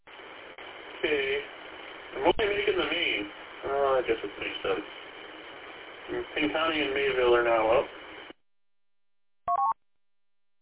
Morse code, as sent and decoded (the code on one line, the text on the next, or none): none